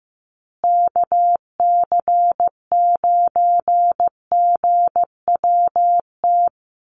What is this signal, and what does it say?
Morse code, heard as a keyed tone, about 15 wpm, 700 Hz. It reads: KC9GWT
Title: USA Call Signs Repeated 3x 15wpm